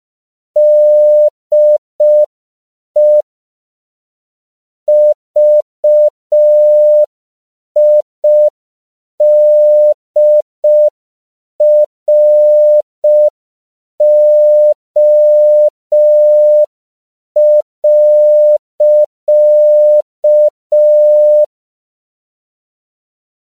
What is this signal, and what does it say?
Morse code, heard as a keyed tone, about 5 wpm, 600 Hz. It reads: DE VIDRO.
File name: CW / 5PPM